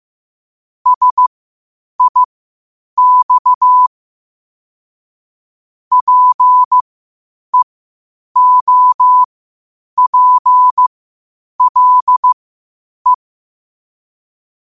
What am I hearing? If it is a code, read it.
Morse code, heard as a keyed tone, about 15 wpm, spaced as for 8 wpm, 1000 Hz. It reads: SIX PEOPLE
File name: qotdfiles / 8wpm